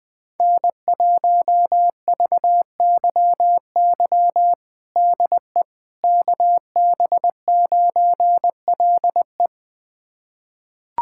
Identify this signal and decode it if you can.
Morse code, heard as a keyed tone, about 20 wpm, 700 Hz. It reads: N1VYY DE KB9LE
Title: USA Callsign DE Callsign - 5 Characters - 20wpm